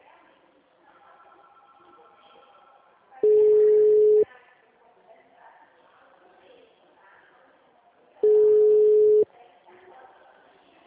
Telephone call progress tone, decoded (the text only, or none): ringback tone